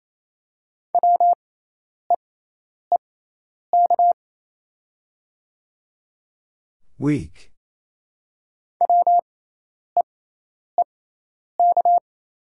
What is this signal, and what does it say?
Morse code, heard as a keyed tone, about 28 wpm, spaced as for 10 wpm, 690 Hz. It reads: WEEK WEEK
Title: 6x Character Spacing - Top 500 Words - 28wpm